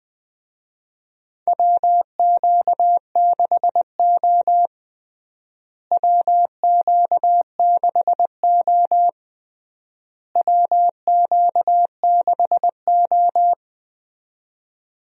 Morse code, heard as a keyed tone, about 20 wpm, 700 Hz. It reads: WQ6O WQ6O WQ6O
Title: USA Call Signs - 4 Characters - Repeated 3x - 20wpm